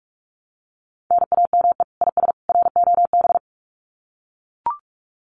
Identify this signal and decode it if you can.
Morse code, heard as a keyed tone, about 44 wpm, 700 Hz. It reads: NAME IS ROB
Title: Common QSO Phrases 45wpm